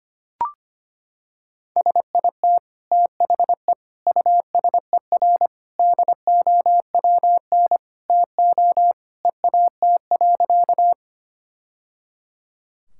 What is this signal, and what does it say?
Morse code, heard as a keyed tone, about 25 wpm, 700 Hz. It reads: SIT THE USER DOWN TO EAT.
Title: Sentences from Top 1000 Words - Review Part 5 - 25wpm